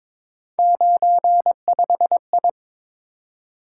Morse code, heard as a keyed tone, about 26 wpm, 700 Hz. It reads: 95I
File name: Speed-Racing - Three Letter-Number Combinations - 22wpm